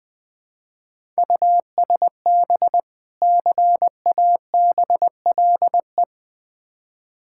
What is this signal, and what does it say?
Morse code, heard as a keyed tone, about 20 wpm, 700 Hz. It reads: USB CABLE
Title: Difficult Words with Hints - 20wpm